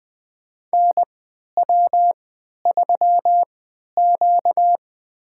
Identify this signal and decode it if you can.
Morse code, heard as a keyed tone, about 20 wpm, spaced as for 11 wpm, 700 Hz. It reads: NW3Q
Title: USA Call Signs - 4 Characters - 3x Character Spacing - 20wpm